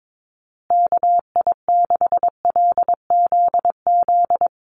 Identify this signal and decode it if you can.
Morse code, heard as a keyed tone, about 22 wpm, 700 Hz. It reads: KI6LZZ
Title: USA Call Signs Repeated 3x 22wpm